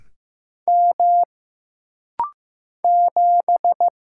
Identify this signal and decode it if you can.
Morse code, heard as a keyed tone, about 15 wpm, 700 Hz. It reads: M 7
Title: Single Letter-Number Alt - Rapid-Fire 15wpm